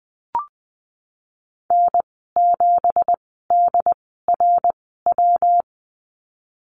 Morse code, heard as a keyed tone, about 20 wpm, spaced as for 15 wpm, 700 Hz. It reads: N7DRW
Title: 167 Lesson 27g - Number 7 - Call Signs 20wpm